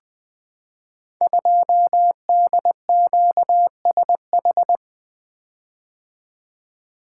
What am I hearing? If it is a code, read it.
Morse code, heard as a keyed tone, about 21 wpm, 700 Hz. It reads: 2DQSH
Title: Code Groups - 5 Letter:Numbers - 20wpm